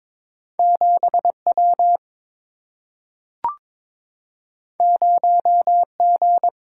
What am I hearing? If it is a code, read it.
Morse code, heard as a keyed tone, about 23 wpm, 700 Hz. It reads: 7 W 0 G
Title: Single-Letter Single-Number 22wpm